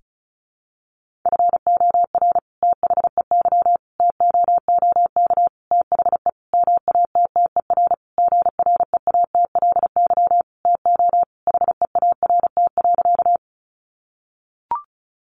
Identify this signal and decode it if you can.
Morse code, heard as a keyed tone, about 34 wpm, 700 Hz. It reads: FOR THEY TOOK THE MATTER GREATLY TO HEART.
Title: Sentences from Top 1400 Words 35wpm